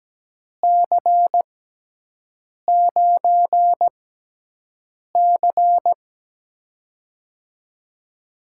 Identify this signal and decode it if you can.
Morse code, heard as a keyed tone, about 17 wpm, spaced as for 6 wpm, 700 Hz. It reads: C9C
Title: USA Call Signs - 4 Characters - 6x Character Spacing - 17wpm